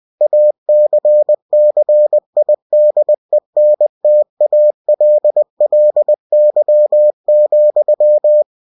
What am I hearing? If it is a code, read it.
Morse code, heard as a keyed tone, about 20 wpm, 600 Hz. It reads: ACCIDENTALLY,
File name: cw-books / winnie-the-pooh / winnie-20wpm-page-0023